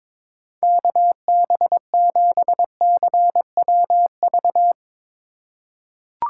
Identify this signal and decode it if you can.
Morse code, heard as a keyed tone, about 21 wpm, 700 Hz. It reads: KB7CWV
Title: USA Call Signs - 6 Characters - 4x Character Spacing - 22wpm